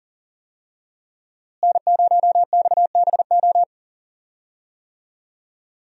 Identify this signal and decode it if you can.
Morse code, heard as a keyed tone, about 39 wpm, 700 Hz. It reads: N0XBO
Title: USA Call Signs - 5 Characters - Repeated 2x - 40wpm